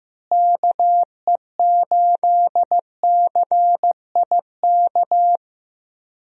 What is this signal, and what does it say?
Morse code, heard as a keyed tone, about 15 wpm, 700 Hz. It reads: KE8CIK
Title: USA Call Signs - 6 Characters - Repeated 3x - 15wpm